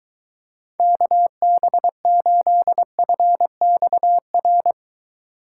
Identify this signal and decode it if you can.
Morse code, heard as a keyed tone, about 22 wpm, 700 Hz. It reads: KB8FXR